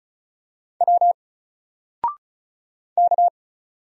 Morse code, heard as a keyed tone, about 34 wpm, 700 Hz. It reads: W K
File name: Single-Letter-Number-35wpm